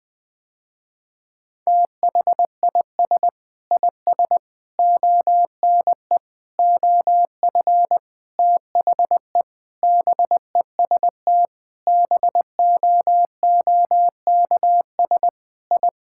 Morse code, heard as a keyed tone, about 21 wpm, 700 Hz. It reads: THIS IS ONE OF THE BEST BOOKS I
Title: Sentences with 5-Word Phrases - Spoken Beginning - 20wpm